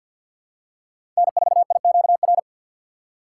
Morse code, heard as a keyed tone, about 52 wpm, 700 Hz. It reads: N4IQR
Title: USA Call Signs - 5 Characters - 6x Character Spacing - 50wpm